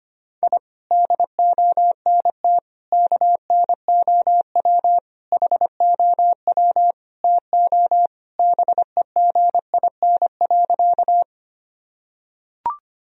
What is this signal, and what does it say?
Morse code, heard as a keyed tone, about 25 wpm, 700 Hz. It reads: I DONT KNOW HOW TO BEGIN.
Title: Sentences from Top 1000 Words - Review Part 1 - 25wpm